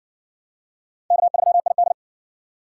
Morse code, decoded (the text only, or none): K3ID